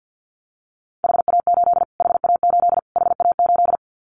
Morse code, heard as a keyed tone, about 50 wpm, 700 Hz. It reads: HA8 HA8 HA8
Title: Callsign Prefixes for Top 25 DXCC - Repeated 3x - 50wpm